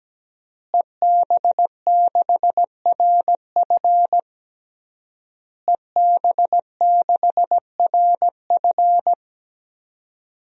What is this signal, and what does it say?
Morse code, heard as a keyed tone, about 17 wpm, 700 Hz. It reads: EB6RF EB6RF
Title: DXCC-100 Call Signs - 5 Characters - Repeated 3x - 17wpm